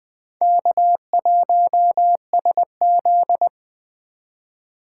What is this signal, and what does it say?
Morse code, heard as a keyed tone, about 20 wpm, 700 Hz. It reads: K1SZ
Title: USA Call Signs - 2x Character Spacing - 20wpm